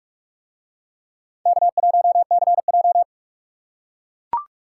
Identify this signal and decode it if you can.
Morse code, heard as a keyed tone, about 41 wpm, 700 Hz. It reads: K1CJ